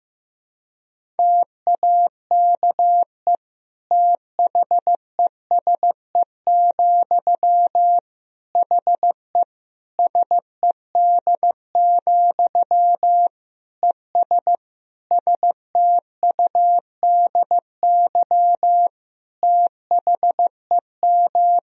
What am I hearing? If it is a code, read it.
Morse code, heard as a keyed tone, about 16 wpm, 700 Hz. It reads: TAKE THESE, HE SED, ES STUDY THEM
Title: Sentences from Top 300 Words and Abbreviations 15wpm